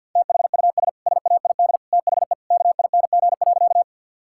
Morse code, heard as a keyed tone, about 50 wpm, 700 Hz. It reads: THUS SAID THE KING.
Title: Sentences from Top 1000 Words - Review Part 5 - 50wpm